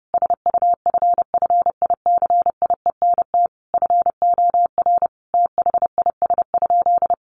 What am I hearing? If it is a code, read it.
Morse code, heard as a keyed tone, about 30 wpm, 700 Hz. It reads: SUFFICIENT FOR THIS?
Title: Sentences from Top 1400 Words 30wpm